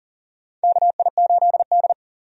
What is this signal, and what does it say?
Morse code, heard as a keyed tone, about 40 wpm, 700 Hz. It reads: KI8D
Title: USA Call Signs - 4 Characters - 40wpm